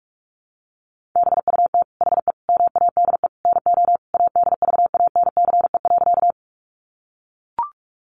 Morse code, heard as a keyed tone, about 44 wpm, 700 Hz. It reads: BUT HE MADE NO ADVANCE.
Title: Sentences from Top 1400 Words 45wpm